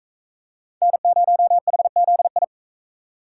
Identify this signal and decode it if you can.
Morse code, heard as a keyed tone, about 41 wpm, 700 Hz. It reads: N0HZI